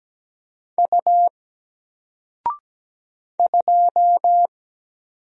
Morse code, heard as a keyed tone, about 17 wpm, 700 Hz. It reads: U 2
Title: Single Character (Letter:Number:Punctuation:Prosign) 17wpm